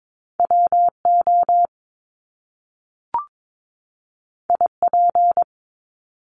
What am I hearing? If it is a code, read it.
Morse code, heard as a keyed tone, about 22 wpm, 700 Hz. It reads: WO IP
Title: Top Two Letter Combinations in Words 22wpm